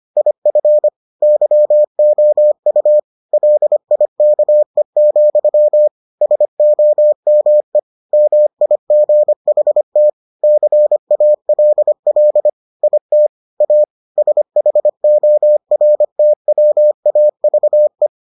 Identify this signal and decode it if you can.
Morse code, heard as a keyed tone, about 25 wpm, 600 Hz. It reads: IF YOU LIKE, SOME MIGHT CALL IT A SHORTWAVE